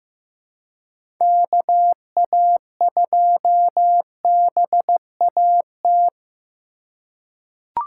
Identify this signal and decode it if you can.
Morse code, heard as a keyed tone, about 15 wpm, 700 Hz. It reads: KA2BAT
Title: USA Call Signs - 6 Characters - 2x Character Spacing - 15wpm